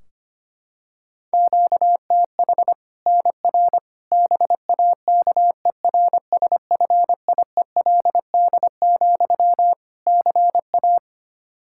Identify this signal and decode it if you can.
Morse code, heard as a keyed tone, about 25 wpm, 700 Hz. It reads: QTH NR BAKERSFIELD, CA
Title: K1USN SST QSO Practice 25wpm